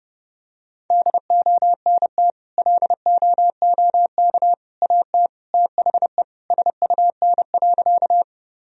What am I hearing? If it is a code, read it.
Morse code, heard as a keyed tone, about 32 wpm, 700 Hz. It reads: DONT LOOK AT THE SUN.
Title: Sentences from Top 1000 Words - Review Part 1 - 30wpm